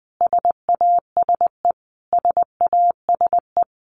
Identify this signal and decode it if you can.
Morse code, heard as a keyed tone, about 20 wpm, 700 Hz. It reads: SASE SASE
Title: Abbreviations Repeated 3x - 20wpm